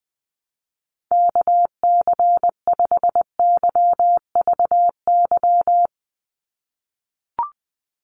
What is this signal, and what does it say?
Morse code, heard as a keyed tone, about 20 wpm, 700 Hz. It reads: KC5YVY
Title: USA Call Signs 20wpm